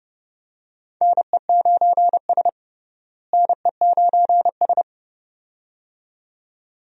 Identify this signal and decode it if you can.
Morse code, heard as a keyed tone, about 30 wpm, 700 Hz. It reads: NE9S NE9S
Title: USA Call Signs - 4 Characters - Repeated 2x - 30wpm